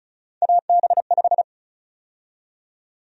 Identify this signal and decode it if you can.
Morse code, heard as a keyed tone, about 36 wpm, 700 Hz. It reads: AB5
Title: USA Call Sign Prefixes - 35wpm